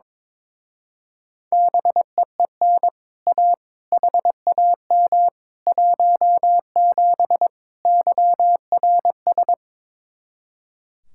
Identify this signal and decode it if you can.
Morse code, heard as a keyed tone, about 22 wpm, 700 Hz. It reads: BEEN A HAM 17 YRS
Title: Speed-Racing Common QSO Phrases Encore - 22wpm